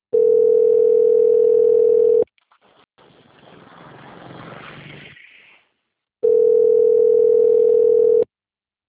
Ringback tone (North America), 440 + 480 Hz.